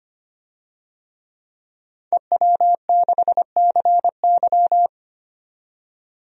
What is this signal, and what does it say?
Morse code, heard as a keyed tone, about 25 wpm, 700 Hz. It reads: EW6CY
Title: DXCC-100 Call Signs - 5 Characters - 25wpm